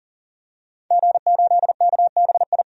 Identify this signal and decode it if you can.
Morse code, heard as a keyed tone, about 40 wpm, 700 Hz. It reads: G8KBI
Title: DXCC-100 Call Signs - 5 Characters - 40wpm